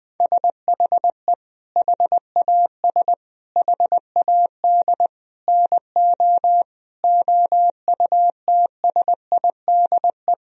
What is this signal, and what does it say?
Morse code, heard as a keyed tone, about 21 wpm, 700 Hz. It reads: SHE HAS HAD NO OUTSIDE
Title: Sentences from Top 1000 Words - Review Part 4 - 20wpm